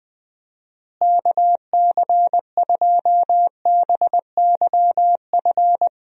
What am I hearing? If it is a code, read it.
Morse code, heard as a keyed tone, about 21 wpm, 700 Hz. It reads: KC2BYF
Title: USA Call Signs - 6 Characters - Repeated 3x - 20wpm